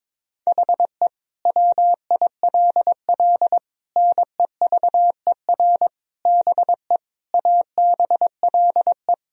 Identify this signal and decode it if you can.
Morse code, heard as a keyed tone, about 22 wpm, 700 Hz. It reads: HE WILL NEVER BE ABLE
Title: Sentences from Top 1000 Words - Review Part 2 - 22wpm